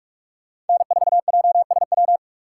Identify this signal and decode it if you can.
Morse code, heard as a keyed tone, about 44 wpm, 700 Hz. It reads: N4JSW
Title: USA Call Signs - 5 Characters - 6x Character Spacing - 45wpm